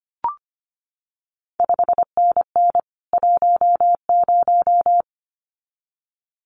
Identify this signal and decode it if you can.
Morse code, heard as a keyed tone, about 25 wpm, 700 Hz. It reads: 5NN 10
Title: CQ WW DX Contest Exchange Encore - 25wpm